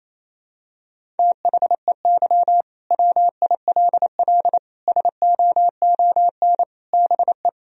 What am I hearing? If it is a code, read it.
Morse code, heard as a keyed tone, about 28 wpm, 700 Hz. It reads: THEY WILL SOON BE
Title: Sentences from Top 1000 Words - Review Part 4 - 28wpm